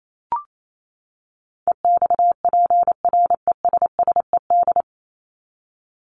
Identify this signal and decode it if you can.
Morse code, heard as a keyed tone, about 28 wpm, 700 Hz. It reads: EXPRESSED
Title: ED Suffix - Top 100 Words with Base Words 28wpm